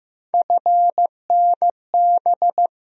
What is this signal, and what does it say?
Morse code, heard as a keyed tone, about 15 wpm, 700 Hz. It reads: FNB
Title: USA Call Sign Postfixes - 3 Letters 15wpm